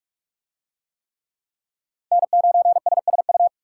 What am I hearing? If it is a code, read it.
Morse code, heard as a keyed tone, about 44 wpm, 700 Hz. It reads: N9SSU